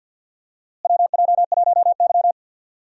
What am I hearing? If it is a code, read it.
Morse code, heard as a keyed tone, about 51 wpm, 700 Hz. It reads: WJ1Y
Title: USA Call Signs - 4 Characters - Repeated 3x - 50wpm